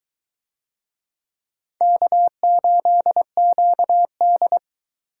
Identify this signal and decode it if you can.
Morse code, heard as a keyed tone, about 23 wpm, 700 Hz. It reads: K8QD